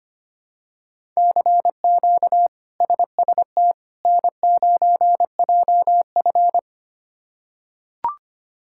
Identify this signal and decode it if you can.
Morse code, heard as a keyed tone, about 25 wpm, 700 Hz. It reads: CQ SST N9JF
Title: K1USN SST QSO Practice - 25wpm